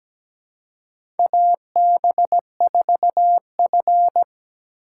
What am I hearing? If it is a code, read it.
Morse code, heard as a keyed tone, about 17 wpm, 700 Hz. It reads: AB4F
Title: USA Call Signs - 4 Characters - 4x Character Spacing - 17wpm